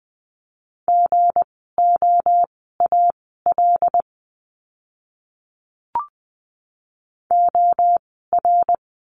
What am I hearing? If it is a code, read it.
Morse code, heard as a keyed tone, about 20 wpm, spaced as for 14 wpm, 700 Hz. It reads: GOAL OR
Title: 160 Lesson 26f - Letter g - Words 20wpm